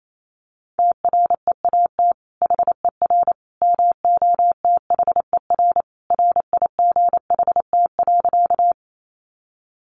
Morse code, heard as a keyed tone, about 28 wpm, 700 Hz. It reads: TREAT HER MOTHER RIGHT.
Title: Sentences from Top 1300 Words 28wpm